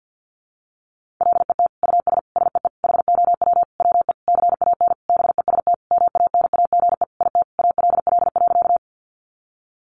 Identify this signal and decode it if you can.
Morse code, heard as a keyed tone, about 50 wpm, 700 Hz. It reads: LET US SEE HOW WE CAN BEST MANAGE IT ALL.
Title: Sentences from Top 1400 Words 50wpm